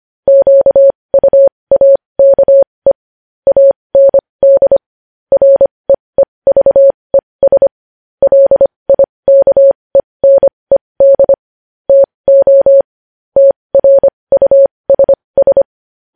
Morse code, heard as a keyed tone, about 25 wpm, spaced as for 20 wpm, 550 Hz. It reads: QUAKE AND REEVES LIKENED TO TRUSS
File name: podcast / News headlines-1x-25x20